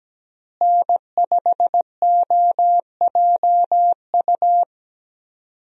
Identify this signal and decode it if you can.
Morse code, heard as a keyed tone, about 17 wpm, 700 Hz. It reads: N5OJU
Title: USA Call Signs - 5 Characters - Repeated 2x - 17wpm